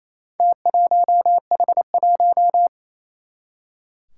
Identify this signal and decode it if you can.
Morse code, heard as a keyed tone, about 28 wpm, 700 Hz. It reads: T1H1
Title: Four Challenging Letter-Number Combinations 28wpm